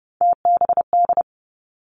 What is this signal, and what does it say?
Morse code, heard as a keyed tone, about 30 wpm, 700 Hz. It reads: TBD
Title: Three Challenging Letter-Number Combinations 30wpm